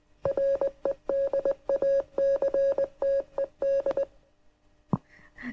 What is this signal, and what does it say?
Morse code, heard as a keyed tone, about 21 wpm, 510 Hz. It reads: REDACTED